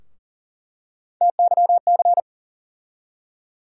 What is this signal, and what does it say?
Morse code, heard as a keyed tone, about 40 wpm, 700 Hz. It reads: TYC
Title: USA Call Sign Postfixes - 3 Letters 40wpm